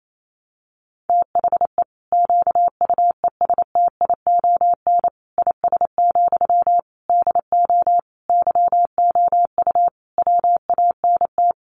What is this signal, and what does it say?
Morse code, heard as a keyed tone, about 28 wpm, 700 Hz. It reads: THE QUESTION IS, DO YOU WANT
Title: Sentences from Top 1000 Words 28wpm